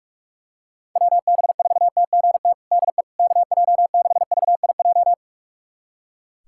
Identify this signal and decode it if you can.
Morse code, heard as a keyed tone, about 46 wpm, 700 Hz. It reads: WB4TGT DE KJ6VIJ
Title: USA Callsign DE Callsign - 6 Characters - 45wpm